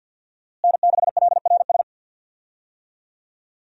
Morse code, heard as a keyed tone, about 53 wpm, 700 Hz. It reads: N6LRS
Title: USA Call Signs - 5 Characters - Repeated 3x - 50wpm